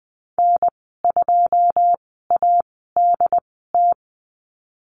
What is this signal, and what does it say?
Morse code, heard as a keyed tone, about 20 wpm, spaced as for 15 wpm, 700 Hz. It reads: N2ADT
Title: 088 Lesson 15g - Number 5 - Call Signs 20wpm